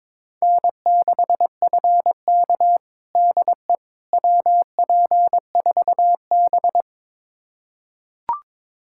Morse code, heard as a keyed tone, about 22 wpm, 700 Hz. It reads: N6FK DE WP4B
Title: USA Callsign DE Callsign - 4 Characters - 22wpm